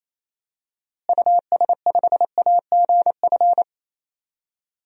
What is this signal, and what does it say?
Morse code, heard as a keyed tone, about 28 wpm, 700 Hz. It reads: US5AGF
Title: DXCC-100 Call Signs - 6 Characters - Repeated 2x - 28wpm